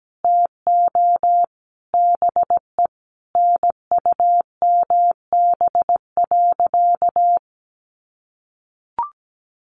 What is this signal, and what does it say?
Morse code, heard as a keyed tone, about 17 wpm, 700 Hz. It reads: TO BE NUMB.
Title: A Sound Of Thunder 17wpm